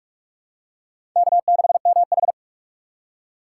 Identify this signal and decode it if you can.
Morse code, heard as a keyed tone, about 42 wpm, 700 Hz. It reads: K6MH
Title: USA Call Signs - 4 Characters - 4x Character Spacing - 45wpm